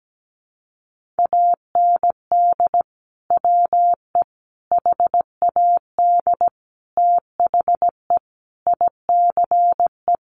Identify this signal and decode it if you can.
Morse code, heard as a keyed tone, about 17 wpm, 700 Hz. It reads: AND WE HAD THE ICE
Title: Sentences from Top 1500 Words 17wpm